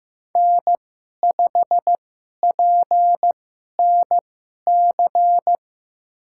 Morse code, heard as a keyed tone, about 15 wpm, spaced as for 11 wpm, 700 Hz. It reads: N5PNC
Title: USA Call Signs - 5 Characters - 2x Character Spacing - 15wpm